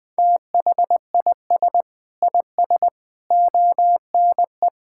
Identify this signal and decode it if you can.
Morse code, heard as a keyed tone, about 20 wpm, 700 Hz. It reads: THIS IS ONE
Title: Sentences with 4-Word Phrases - 20wpm